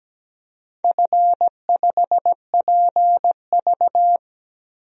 Morse code, heard as a keyed tone, about 17 wpm, 700 Hz. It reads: F5PV
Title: DXCC-100 Call Signs - 4 Characters - 17wpm